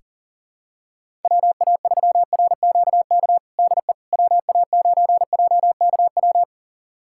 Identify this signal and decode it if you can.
Morse code, heard as a keyed tone, about 37 wpm, 690 Hz. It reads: WA3RQK DE WA9JKW
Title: USA Callsign DE Callsign - 6 Characters - 40wpm